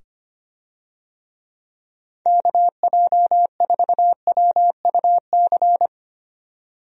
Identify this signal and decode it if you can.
Morse code, heard as a keyed tone, about 25 wpm, 700 Hz. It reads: KJ4WUC
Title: USA Call Signs - 6 Characters - 25wpm